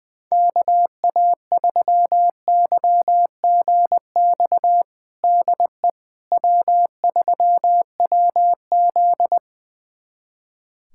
Morse code, heard as a keyed tone, about 21 wpm, 700 Hz. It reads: KA3YGX DE W3WZ
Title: USA Callsign DE Callsign 20wpm